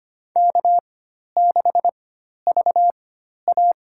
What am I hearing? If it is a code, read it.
Morse code, heard as a keyed tone, about 25 wpm, spaced as for 12 wpm, 700 Hz. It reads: K6VA
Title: USA Call Signs - 4x Character Spacing - 25wpm